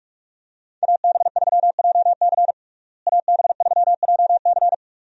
Morse code, heard as a keyed tone, about 42 wpm, 700 Hz. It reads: AB3JC AB3JC
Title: USA Call Signs - 5 Characters - Repeated 2x - 45wpm